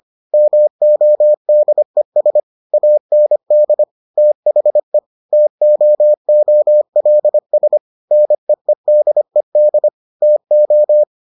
Morse code, heard as a keyed tone, about 25 wpm, 600 Hz. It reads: MODES AND THE TOOLS NEEDED TO